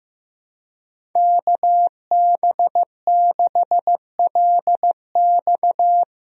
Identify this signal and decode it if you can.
Morse code, heard as a keyed tone, about 15 wpm, 700 Hz. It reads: KB6LX
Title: USA Call Signs - 5 Characters - Repeated 2x - 15wpm